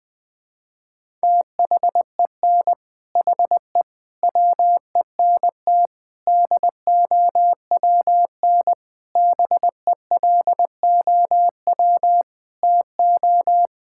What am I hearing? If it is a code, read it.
Morse code, heard as a keyed tone, about 20 wpm, 700 Hz. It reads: THEN HE WENT DOWN BELOW TO
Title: Sentences from Top 1000 Words - Review Part 5 - 20wpm